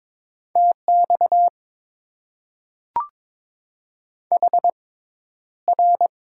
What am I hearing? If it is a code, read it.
Morse code, heard as a keyed tone, about 22 wpm, spaced as for 11 wpm, 700 Hz. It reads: TX HR
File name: Top 90 QSO Elements - 6x Character Spacing - 22wpm